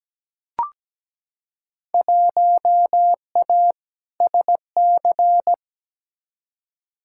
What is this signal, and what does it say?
Morse code, heard as a keyed tone, about 17 wpm, 700 Hz. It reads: 1A SC
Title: Field Day Exchanges - 17wpm